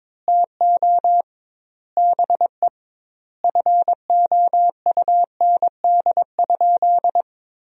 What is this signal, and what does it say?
Morse code, heard as a keyed tone, about 22 wpm, 700 Hz. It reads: TO BE FOUND?
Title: 2x Word-Spacing - Sentences from Top 300 Words - 22wpm